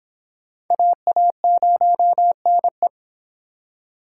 Morse code, heard as a keyed tone, about 26 wpm, 700 Hz. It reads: AA0NE